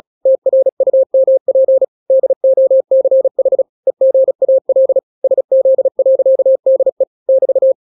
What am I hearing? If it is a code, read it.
Morse code, heard as a keyed tone, about 36 wpm, 530 Hz. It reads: TRUMP DOCH EGAL SZ.DE <BT>
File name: daily / 35wpm